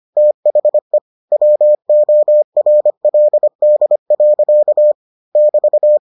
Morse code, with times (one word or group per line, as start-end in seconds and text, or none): 0.2-1.0 THE
1.3-4.9 WORLD.
5.4-6.0 <BT>